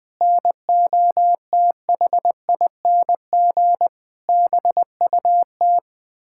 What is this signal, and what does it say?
Morse code, heard as a keyed tone, about 20 wpm, 700 Hz. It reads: NOTHING BUT
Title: Sentences with 9-Word Phrases - 20wpm